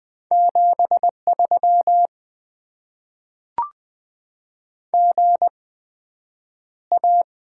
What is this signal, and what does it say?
Morse code, heard as a keyed tone, about 20 wpm, spaced as for 9 wpm, 700 Hz. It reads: 73 GA
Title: Top 20 QSO Elements - 8x Character Spacing - 20wpm